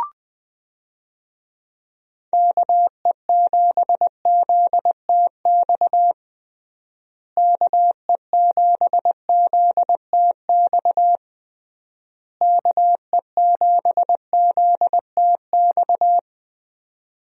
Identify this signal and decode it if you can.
Morse code, heard as a keyed tone, about 20 wpm, 700 Hz. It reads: KE7ZTX KE7ZTX KE7ZTX
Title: USA Call Signs - 6 Characters - Repeated 3x - 20wpm